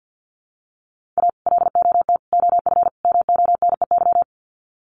Morse code, heard as a keyed tone, about 50 wpm, 700 Hz. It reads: A LOT OF MONEY
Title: Sets of 4 Words 50wpm